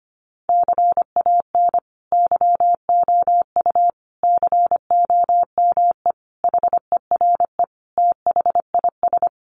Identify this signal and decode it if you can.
Morse code, heard as a keyed tone, about 25 wpm, 700 Hz. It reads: CAN YOU COME HERE THIS
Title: Sentences from Top 1500 Words 25wpm